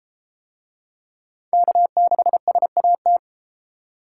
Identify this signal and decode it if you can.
Morse code, heard as a keyed tone, about 32 wpm, 700 Hz. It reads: K6SAT